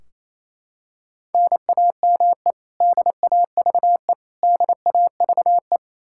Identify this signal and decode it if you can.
Morse code, heard as a keyed tone, about 28 wpm, 700 Hz. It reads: NAME DAVE DAVE
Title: Practice QSO - Exchange 1 - Beginning - 28wpm